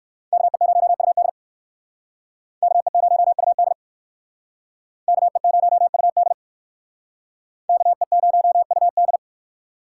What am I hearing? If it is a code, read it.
Morse code, heard as a keyed tone, about 64 wpm, 700 Hz. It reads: KE0UD KE0UD KE0UD KE0UD